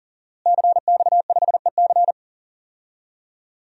Morse code, heard as a keyed tone, about 37 wpm, 700 Hz. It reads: CX5EC
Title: DXCC-100 Call Signs - 5 Characters - Repeated 3x - 40wpm